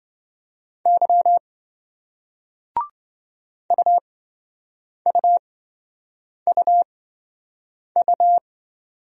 Morse code, heard as a keyed tone, about 27 wpm, spaced as for 7 wpm, 700 Hz. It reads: Y UUUU